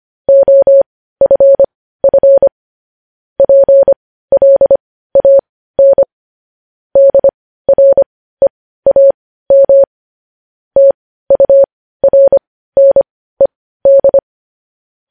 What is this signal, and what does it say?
Morse code, heard as a keyed tone, about 25 wpm, spaced as for 15 wpm, 550 Hz. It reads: OFF PLAN DREAM TURNED
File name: podcast / News headlines-1x-25x15